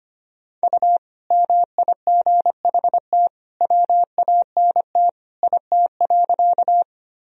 Morse code, heard as a keyed tone, about 25 wpm, 700 Hz. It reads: U MIGHT WANT IT.
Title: Sentences from Top 300 Words and Abbreviations 25wpm